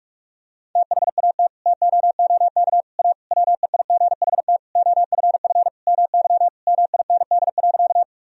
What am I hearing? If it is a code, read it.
Morse code, heard as a keyed tone, about 42 wpm, 700 Hz. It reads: THAT TOOK A WEIGHT OFF MY MIND.
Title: Sentences from Top 1000 Words - Review Part 3 - 45wpm